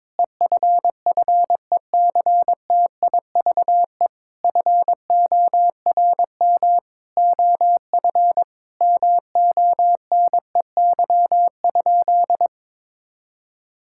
Morse code, heard as a keyed tone, about 22 wpm, 700 Hz. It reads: EFFECTIVE FORM OF MONEY?
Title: Sentences from Top 1000 Words - Review Part 6 - 22wpm